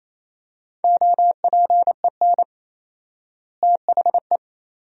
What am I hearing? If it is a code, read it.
Morse code, heard as a keyed tone, about 27 wpm, 700 Hz. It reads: OPEN THE
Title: 4x Word-Spacing - Sentences from Top 300 Words - 28wpm